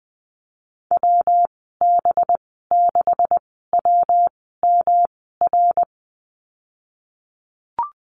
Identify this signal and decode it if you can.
Morse code, heard as a keyed tone, about 20 wpm, spaced as for 15 wpm, 700 Hz. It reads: WB6WMR
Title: 186 Lesson 30g - Letter b - Call Signs 20wpm